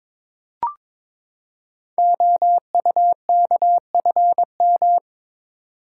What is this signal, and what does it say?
Morse code, heard as a keyed tone, about 22 wpm, 700 Hz. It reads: OUKFM
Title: Code Groups - 5 Letters - 22wpm